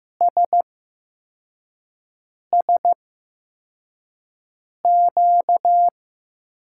Morse code, heard as a keyed tone, about 15 wpm, spaced as for 4 wpm, 700 Hz. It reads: SSQ
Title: USA Call Signs - 6 Characters - 8x Character Spacing - 15wpm